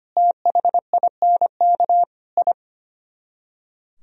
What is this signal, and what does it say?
Morse code, heard as a keyed tone, about 30 wpm, 700 Hz. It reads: THINK I
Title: Speed-Racing - Sets of 5 Words 25wpm